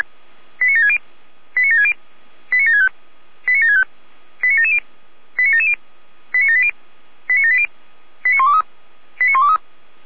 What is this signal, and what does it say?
Five-tone selective call (ZVEI), heard as quick runs of five tones, tones of about 80 ms. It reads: 89760 89760 89765 89765 89770 89770 89780 89780 89123 89123